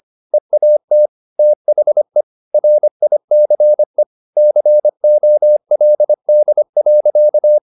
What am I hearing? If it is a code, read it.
Morse code, heard as a keyed tone, about 25 wpm, 600 Hz. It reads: EAT THE RICE COLD.
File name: the old man and the sea part 1 25wpm